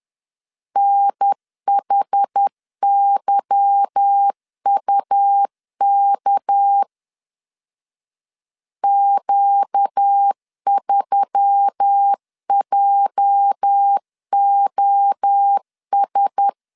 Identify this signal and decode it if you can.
Morse code, heard as a keyed tone, about 10 wpm, 800 Hz. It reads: NHYUK Q3JOS
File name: LEZIONI / 8 YJ45 / 8 YJ 10wpm 2651